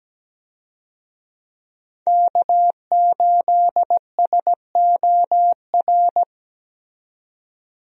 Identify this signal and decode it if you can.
Morse code, heard as a keyed tone, about 17 wpm, 700 Hz. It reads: K8SOR
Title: USA Call Signs - 5 Characters - 17wpm